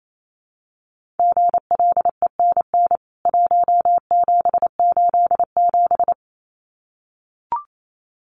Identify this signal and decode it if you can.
Morse code, heard as a keyed tone, about 28 wpm, 700 Hz. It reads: GLENN 1787
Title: CWT Exchange 28wpm